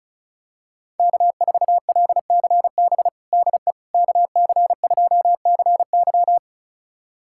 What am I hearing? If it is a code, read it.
Morse code, heard as a keyed tone, about 34 wpm, 700 Hz. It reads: K4LCB DE KC2CY
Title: USA Callsign DE Callsign - 5 Characters - 35wpm